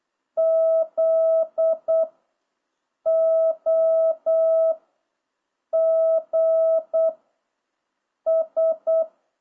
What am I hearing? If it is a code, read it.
Morse code, heard as a keyed tone, about 8 wpm, 660 Hz. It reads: Z O G S